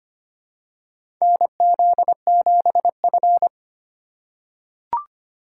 Morse code, heard as a keyed tone, about 25 wpm, 700 Hz. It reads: NZ7F
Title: USA Call Signs - 4 Characters - 2x Character Spacing - 25wpm